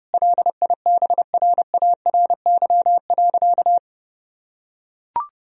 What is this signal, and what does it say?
Morse code, heard as a keyed tone, about 32 wpm, 700 Hz. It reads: LIBRARY.
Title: Sentences from Top 1000 Words - Review Part 4 - 30wpm